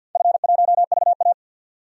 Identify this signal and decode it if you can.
Morse code, heard as a keyed tone, about 51 wpm, 700 Hz. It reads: F1VA
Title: DXCC-100 Call Signs - 4 Characters - 50wpm